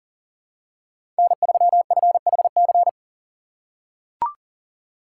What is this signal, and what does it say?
Morse code, heard as a keyed tone, about 40 wpm, 700 Hz. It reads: N3FHC
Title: USA Call Signs - 5 Characters - 3x Character Spacing - 40wpm